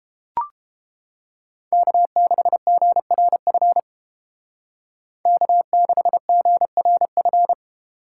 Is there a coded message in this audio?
Yes: Morse code from 1.7 to 7.5 s, heard as a keyed tone, about 29 wpm, 700 Hz. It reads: K6GRF K6GRF